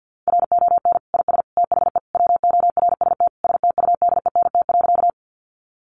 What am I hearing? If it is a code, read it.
Morse code, heard as a keyed tone, about 48 wpm, 700 Hz. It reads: RON IS THE WORST STUDENT.
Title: Additional Easy Phrases 50wpm